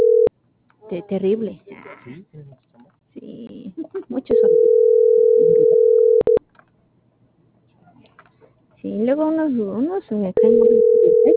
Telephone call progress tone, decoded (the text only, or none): ringback tone